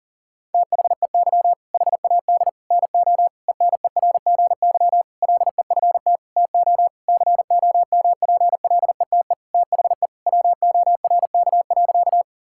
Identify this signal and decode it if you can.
Morse code, heard as a keyed tone, about 40 wpm, 700 Hz. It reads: THEY HAD NO ENERGY LEFT TO COMPLETE THE WORK.
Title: Sentences from Top 1000 Words - Review Part 6 - 40wpm